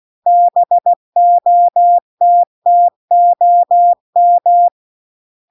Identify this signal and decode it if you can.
Morse code, heard as a keyed tone, about 16 wpm, 700 Hz. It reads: BOTTOM
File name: CW-The-Big-Twoheaded-River-2-16WPM